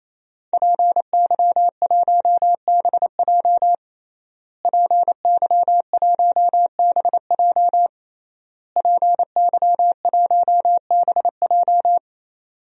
Morse code, heard as a keyed tone, about 27 wpm, 700 Hz. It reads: PY1BJ PY1BJ PY1BJ
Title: DXCC-100 Call Signs - 5 Characters - Repeated 3x - 28wpm